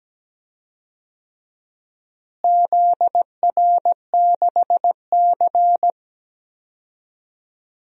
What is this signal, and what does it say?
Morse code, heard as a keyed tone, about 17 wpm, 700 Hz. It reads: ZR6C
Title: DXCC-100 Call Signs - 4 Characters - Repeated 3x - 17wpm